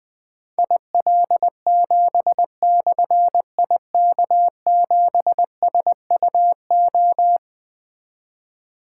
Morse code, heard as a keyed tone, about 20 wpm, 700 Hz. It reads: IL7/IK7SUO
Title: Difficult Call Signs 20wpm